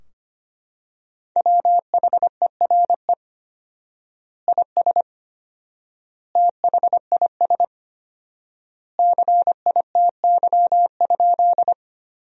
Morse code, heard as a keyed tone, about 25 wpm, 700 Hz. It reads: WHERE IS THIS CITY?
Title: 4x Word-Spacing - Sentences from Top 300 Words - 25wpm